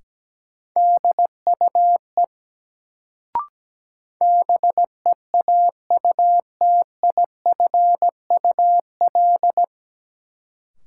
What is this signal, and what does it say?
Morse code, heard as a keyed tone, about 17 wpm, 700 Hz. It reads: DUE BEAUTIFUL
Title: Sentences from Top 900 Words - New Word Drill - 17wpm